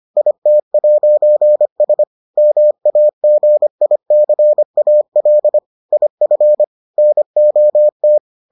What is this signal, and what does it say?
Morse code, heard as a keyed tone, about 25 wpm, 600 Hz. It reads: IT'S MAGICAL IF NOT